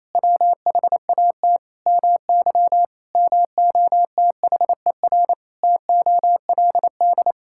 Morse code, heard as a keyed tone, about 28 wpm, 700 Hz. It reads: WHAT MY MOTHER TOLD
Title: Sentences from Top 1000 Words - Review Part 6 - 28wpm